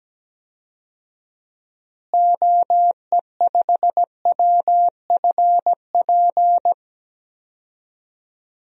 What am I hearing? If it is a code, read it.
Morse code, heard as a keyed tone, about 17 wpm, 700 Hz. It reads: OE5WFP
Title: DXCC-100 Call Signs - 6 Characters - Repeated 2x - 17wpm